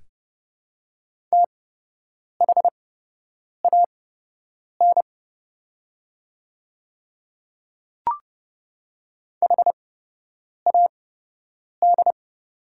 Morse code, heard as a keyed tone, about 29 wpm, spaced as for 8 wpm, 700 Hz. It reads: THAN HAD
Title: 8x Character Spacing - Top 100 Words - 30wpm